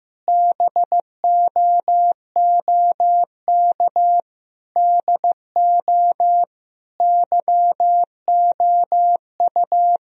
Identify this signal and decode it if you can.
Morse code, heard as a keyed tone, about 15 wpm, 700 Hz. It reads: BOOK DO YOU
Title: Sentences from Top 1000 Words - Review Part 4 - 15wpm